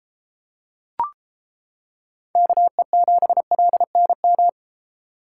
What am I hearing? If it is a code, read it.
Morse code, heard as a keyed tone, about 29 wpm, 700 Hz. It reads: KE7LNM